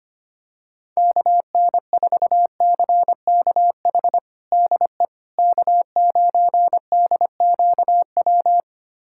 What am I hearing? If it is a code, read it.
Morse code, heard as a keyed tone, about 25 wpm, 700 Hz. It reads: KN4CKH DE K9DQW
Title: USA Callsign DE Callsign 25wpm